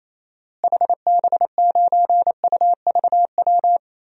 Morse code, heard as a keyed tone, about 28 wpm, 700 Hz. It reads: HB9UVW
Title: DXCC-100 Call Signs - 6 Characters - Repeated 2x - 28wpm